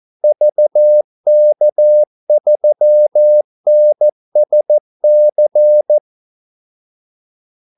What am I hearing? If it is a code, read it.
Morse code, heard as a keyed tone, about 14 wpm, 600 Hz. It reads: VK3NSC
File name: Callsigns-14wpm